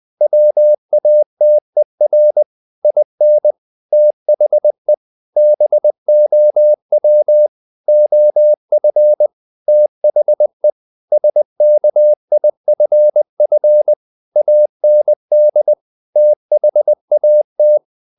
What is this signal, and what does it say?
Morse code, heard as a keyed tone, about 20 wpm, 600 Hz. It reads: WATER IN THE BOW OF THE SKIFF AND THAT
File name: the old man and the sea / the old man and the sea part 1 20wpm